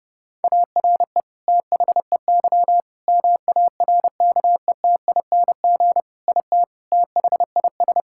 Morse code, heard as a keyed tone, about 30 wpm, 700 Hz. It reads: ARE THEY MARKETING IT THIS
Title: Sentences from Top 1000 Words - Review Part 5 - 30wpm